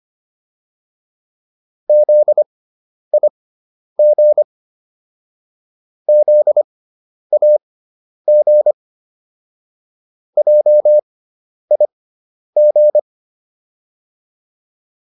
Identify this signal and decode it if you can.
Morse code, heard as a keyed tone, about 25 wpm, spaced as for 10 wpm, 600 Hz. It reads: ZIG ZAG JIG